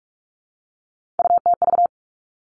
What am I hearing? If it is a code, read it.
Morse code, heard as a keyed tone, about 46 wpm, 700 Hz. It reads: UTV
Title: Three Challenging Letter-Number Combinations 45wpm